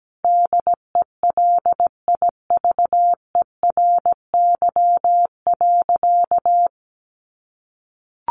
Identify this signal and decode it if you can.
Morse code, heard as a keyed tone, about 17 wpm, 700 Hz. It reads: DELIVERY.
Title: Sentences from Top 1500 Words 17wpm